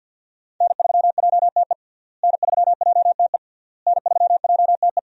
Morse code, heard as a keyed tone, about 51 wpm, 700 Hz. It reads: N3JTE N3JTE N3JTE
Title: USA Call Signs - 5 Characters - Repeated 3x - 50wpm